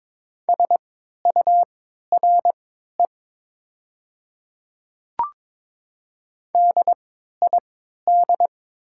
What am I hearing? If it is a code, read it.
Morse code, heard as a keyed tone, about 22 wpm, spaced as for 12 wpm, 700 Hz. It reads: SURE DID
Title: 3x Character Spacing - Top 300 Words - 22wpm